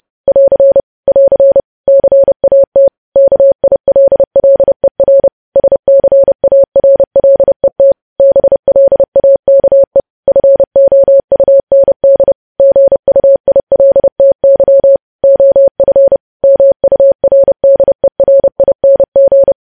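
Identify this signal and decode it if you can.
Morse code, heard as a keyed tone, about 30 wpm, 550 Hz. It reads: <AR> <AR> CAT KILLER SCARLET BLAKE FOUND GUILTY OF MURDERING
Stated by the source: News headlines sent in Morse code at 30.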